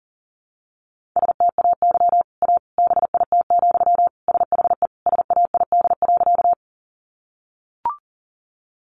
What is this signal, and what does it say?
Morse code, heard as a keyed tone, about 40 wpm, 700 Hz. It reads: STAY A BIT, SHE SAID.
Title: Sentences from Top 900 Words 40wpm